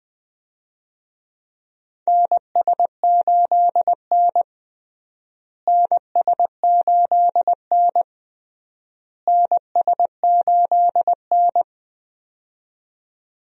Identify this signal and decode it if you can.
Morse code, heard as a keyed tone, about 20 wpm, 700 Hz. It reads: NS8N NS8N NS8N
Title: USA Call Signs - 4 Characters - Repeated 3x - 20wpm